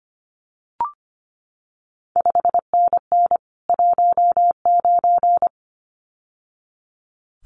Morse code, heard as a keyed tone, about 25 wpm, 700 Hz. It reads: 5NN 19
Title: CQ WW DX Contest Exchange - 25wpm